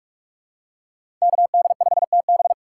Morse code, heard as a keyed tone, about 44 wpm, 700 Hz. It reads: KD5TB
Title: USA Call Signs - 5 Characters - 3x Character Spacing - 45wpm